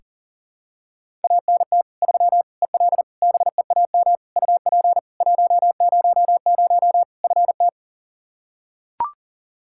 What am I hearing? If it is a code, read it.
Morse code, heard as a keyed tone, about 40 wpm, 700 Hz. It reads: ANT 3 EL BEAM UP 100 FT
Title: K1USN SST QSO Practice 40wpm